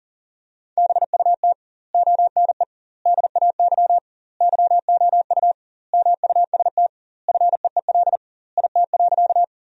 Morse code, heard as a keyed tone, about 40 wpm, 700 Hz. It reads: BUT ONE DAY YOU MUST FEEL IT.
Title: 2x Word-Spacing - Sentences from Top 400 Words - 40wpm